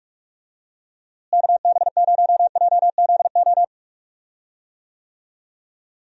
Morse code, heard as a keyed tone, about 46 wpm, 700 Hz. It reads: KB0JZO
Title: USA Call Signs - 6 Characters - Repeated 2x - 45wpm